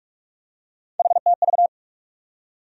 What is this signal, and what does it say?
Morse code, heard as a keyed tone, about 46 wpm, 700 Hz. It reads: HTV
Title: USA Call Sign Postfixes - 3 Letters 45wpm